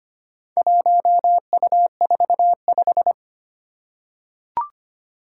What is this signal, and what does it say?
Morse code, heard as a keyed tone, about 25 wpm, 700 Hz. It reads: 1U45
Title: Four Challenging Letter-Number Combinations 25wpm